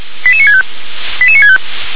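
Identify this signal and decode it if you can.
Five-tone selective call (ZVEI), heard as quick runs of five tones, tones of about 70 ms. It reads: 88965 88965
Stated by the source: ZVEI five-tone selective call received on an SDR.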